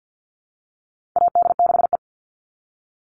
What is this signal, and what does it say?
Morse code, heard as a keyed tone, about 48 wpm, 700 Hz. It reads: AD6E
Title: Call Signs Repeated 3x - 50wpm